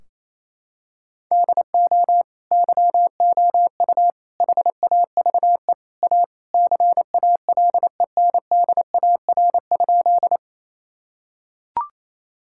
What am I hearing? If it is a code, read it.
Morse code, heard as a keyed tone, about 27 wpm, 700 Hz. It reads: DO YOU HAVE A CALENDAR?
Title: Sentences from Top 1600 Words 28wpm